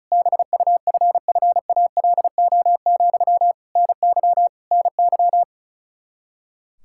Morse code, heard as a keyed tone, about 36 wpm, 700 Hz. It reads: BUFFALO, NY NY
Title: K1USN SST QSO Practice 35wpm